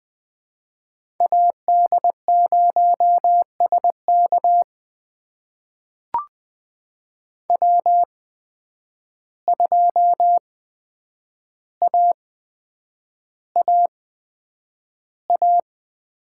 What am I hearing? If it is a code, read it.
Morse code, heard as a keyed tone, about 20 wpm, spaced as for 9 wpm, 700 Hz. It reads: AD0SK W2AAA
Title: USA Call Signs - 5 Characters - 8x Character Spacing - 20wpm